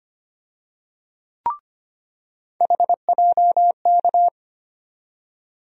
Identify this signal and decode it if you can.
Morse code, heard as a keyed tone, about 25 wpm, 700 Hz. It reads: HJK
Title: USA Call Sign Postfixes - 3 Letters 25wpm